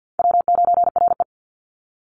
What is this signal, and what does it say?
Morse code, heard as a keyed tone, about 50 wpm, 700 Hz. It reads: W9RE
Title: Easy Call Signs Repeated 2x - 50wpm